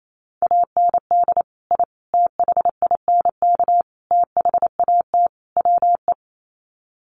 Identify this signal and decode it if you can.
Morse code, heard as a keyed tone, about 28 wpm, 700 Hz. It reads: AND I THINK THAT WE
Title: Sets of 5 Words Encore2 28wpm